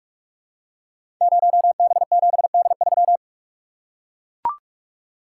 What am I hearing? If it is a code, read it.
Morse code, heard as a keyed tone, about 44 wpm, 690 Hz. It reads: 0B7D3
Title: Code Groups - 5 Letter:Numbers - 45wpm